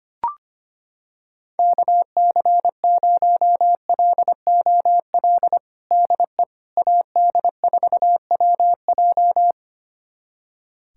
Morse code, heard as a keyed tone, about 25 wpm, 700 Hz. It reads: KC0LOL DE AD4WJ
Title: USA Callsign DE Callsign 25wpm